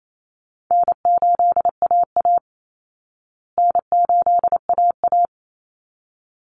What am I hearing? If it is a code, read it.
Morse code, heard as a keyed tone, about 28 wpm, 700 Hz. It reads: N8AA N8AA
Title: Easy Call Signs Repeated 2x - 28wpm